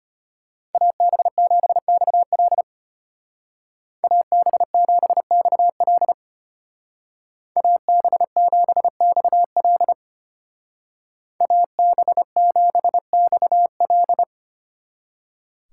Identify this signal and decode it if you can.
Morse code, heard as a keyed tone, about 33 wpm, 700 Hz. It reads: AB7XL AB7XL AB7XL AB7XL